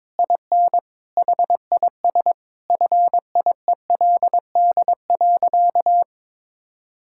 Morse code, heard as a keyed tone, about 22 wpm, 700 Hz. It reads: IN HIS FIELD.
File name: Sentences with 7-Word Phrases - 22wpm